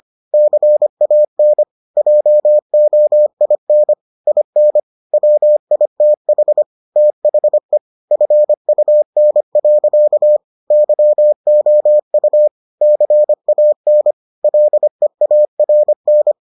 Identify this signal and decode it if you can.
Morse code, heard as a keyed tone, about 25 wpm, 600 Hz. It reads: CAN JOIN IN WITH THE FUN. YOU CAN LEARN